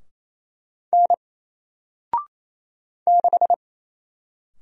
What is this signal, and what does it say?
Morse code, heard as a keyed tone, about 27 wpm, 700 Hz. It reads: N 6
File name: Single-Letter-Number-28wpm